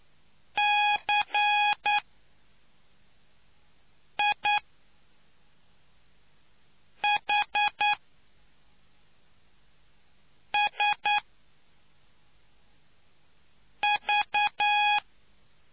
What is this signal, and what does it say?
Morse code, heard as a keyed tone, about 9 wpm, spaced as for 3 wpm, 830 Hz. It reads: CIHSV